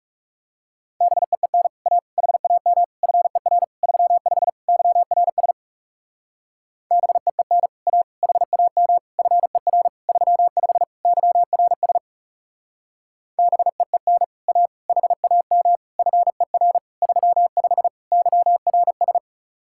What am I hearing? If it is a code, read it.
Morse code, heard as a keyed tone, about 36 wpm, 700 Hz. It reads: BEEN A HAM FER 35 YRS BEEN A HAM FER 35 YRS BEEN A HAM FER 35 YRS
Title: Speed-Racing Common QSO Phrases - 30wpm